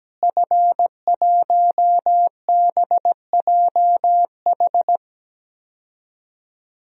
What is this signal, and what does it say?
Morse code, heard as a keyed tone, about 17 wpm, 700 Hz. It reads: F1BJH
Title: DXCC-100 Call Signs - 5 Characters - Repeated 2x - 17wpm